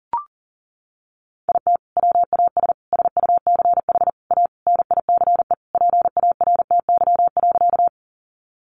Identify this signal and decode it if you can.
Morse code, heard as a keyed tone, about 40 wpm, 700 Hz. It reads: IT WAS SUCH A NICE PARTY.
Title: Sentences from Top 1400 Words 40wpm